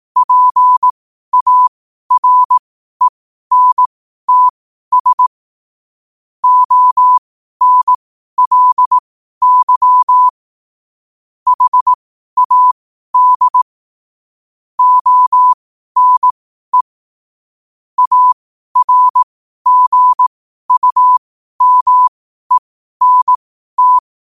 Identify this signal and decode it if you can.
Morse code, heard as a keyed tone, about 18 wpm, spaced as for 12 wpm, 1000 Hz. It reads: PARENTS ONLY HAD ONE ARGUMENT
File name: qotdfiles / 12wpm